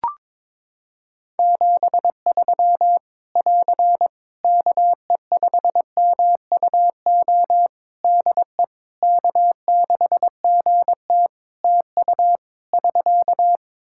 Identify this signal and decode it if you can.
Morse code, heard as a keyed tone, about 22 wpm, 700 Hz. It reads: 73 <AR> KE5MUO DE K6GT TU <SK>
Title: Practice QSO - Exchange 4 - Ending - 22wpm